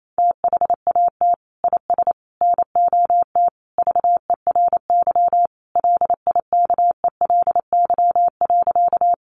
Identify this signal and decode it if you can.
Morse code, heard as a keyed tone, about 28 wpm, 700 Hz. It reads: THAT IS NOT VERY LIKELY.
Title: Sentences from Top 600 Words 28wpm